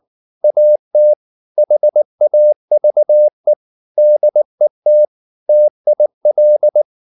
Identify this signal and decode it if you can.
Morse code, heard as a keyed tone, about 19 wpm, 600 Hz. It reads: AT HAVE DET TIL
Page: Ferie 19 wpm